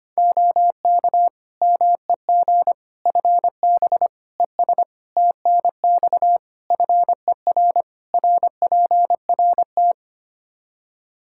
Morse code, heard as a keyed tone, about 25 wpm, 700 Hz. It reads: OK MEG FB ES TNX FER RPRT
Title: Practice QSO - Exchange 2 - Info - 25wpm